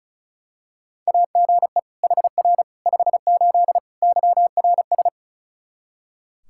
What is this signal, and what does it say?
Morse code, heard as a keyed tone, about 36 wpm, 700 Hz. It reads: AGE HR 58 YRS
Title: Practice QSO - Exchange 3 - More Info - 35wpm